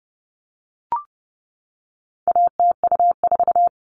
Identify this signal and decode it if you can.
Morse code, heard as a keyed tone, about 30 wpm, 700 Hz. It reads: ATU4
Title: All Cut Numbers - 4-Digit 30wpm